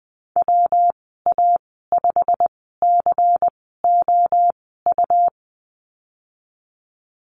Morse code, heard as a keyed tone, about 20 wpm, spaced as for 15 wpm, 700 Hz. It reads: WA5COU
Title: 128 Lesson 21g - Number 6 - Call Signs 20wpm